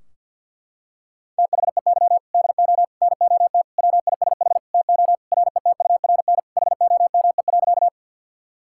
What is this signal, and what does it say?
Morse code, heard as a keyed tone, about 50 wpm, 700 Hz. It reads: THEY DO NOT WISH TO RETURN HOME.
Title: Sentences from Top 1600 Words 50wpm